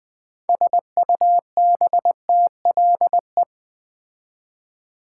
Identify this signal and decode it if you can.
Morse code, heard as a keyed tone, about 20 wpm, 700 Hz. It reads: SUBTLE
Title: Build A Word - 2 Letters At A Time - 6-Letter Words - 20wpm